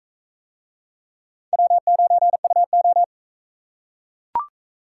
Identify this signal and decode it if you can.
Morse code, heard as a keyed tone, about 36 wpm, 700 Hz. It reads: W9UO